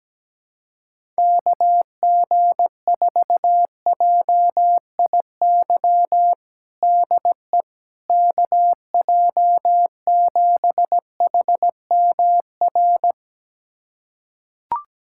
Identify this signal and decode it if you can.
Morse code, heard as a keyed tone, about 17 wpm, 700 Hz. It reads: KG4JIY DE KJ7HMR
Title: USA Callsign DE Callsign 17wpm